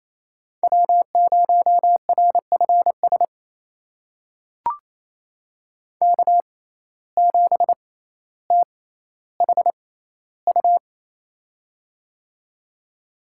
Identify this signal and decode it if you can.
Morse code, heard as a keyed tone, about 28 wpm, spaced as for 14 wpm, 700 Hz. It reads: W0RFS K7THU
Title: USA Call Signs - 5 Characters - 6x Character Spacing - 28wpm